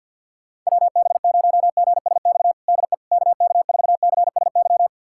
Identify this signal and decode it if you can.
Morse code, heard as a keyed tone, about 53 wpm, 700 Hz. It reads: WB0GSX DE KK4CSY
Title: USA Callsign DE Callsign - 6 Characters - 50wpm